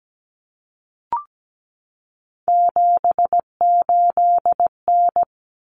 Morse code, heard as a keyed tone, about 17 wpm, 700 Hz. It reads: 78N
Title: Common Cut Numbers - 3-Digit 17wpm